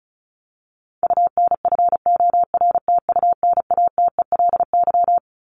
Morse code, heard as a keyed tone, about 34 wpm, 700 Hz. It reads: UNFORTUNATELY
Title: LY Suffix - Top 100 Words with Base Words 35wpm